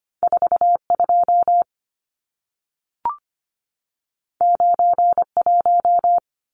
Morse code, heard as a keyed tone, about 25 wpm, 700 Hz. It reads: 42 91
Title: Numbers - 2-Digit 25wpm